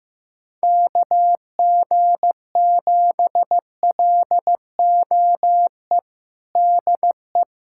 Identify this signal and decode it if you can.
Morse code, heard as a keyed tone, about 15 wpm, 700 Hz. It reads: KG7LOE DE
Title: USA Callsign DE Callsign - 6 Characters - 15wpm